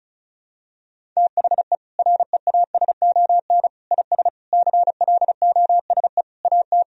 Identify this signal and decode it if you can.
Morse code, heard as a keyed tone, about 36 wpm, 700 Hz. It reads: THE REASON IS CLOSE AT
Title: Sentences from Top 1000 Words - Review Part 1 - 35wpm